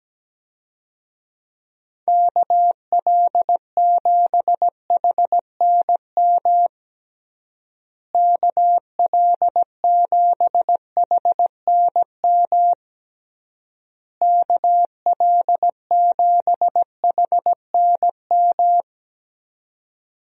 Morse code, heard as a keyed tone, about 17 wpm, 700 Hz. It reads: KL7HNM KL7HNM KL7HNM
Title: USA Call Signs - 6 Characters - Repeated 3x - 17wpm